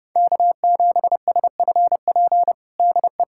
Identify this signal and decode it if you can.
Morse code, heard as a keyed tone, about 30 wpm, 700 Hz. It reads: K7SFP DE
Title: USA Callsign DE Callsign 30wpm